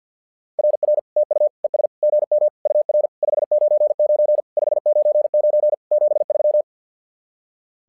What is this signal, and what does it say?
Morse code, heard as a keyed tone, about 50 wpm, 600 Hz. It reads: RR TU ES GM UR 599 599 73
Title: SOTA QSO Practice 50wpm